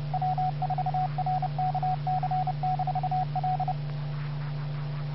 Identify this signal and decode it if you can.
Morse code, heard as a keyed tone, about 30 wpm, 740 Hz, with radio noise. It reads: W4RKC-L
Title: Bluemont, VA Skywarn Repeater - WA4TSC